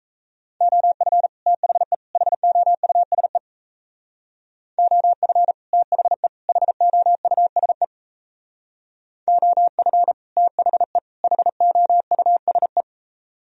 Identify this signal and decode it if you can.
Morse code, heard as a keyed tone, about 35 wpm, 700 Hz. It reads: OF THE HOUSE OF THE HOUSE OF THE HOUSE
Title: Speed-Racing - Sets of 3 Words 28wpm